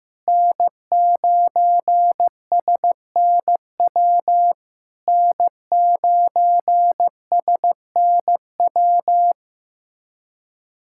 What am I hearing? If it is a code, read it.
Morse code, heard as a keyed tone, about 15 wpm, 700 Hz. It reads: N9SNW N9SNW
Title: SOTA QSO Practice 15wpm